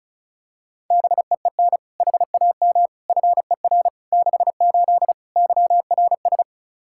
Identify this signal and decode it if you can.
Morse code, heard as a keyed tone, about 34 wpm, 700 Hz. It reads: BEEN HAM FER 68 YRS
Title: Practice QSO - Exchange 3 - More Info - 35wpm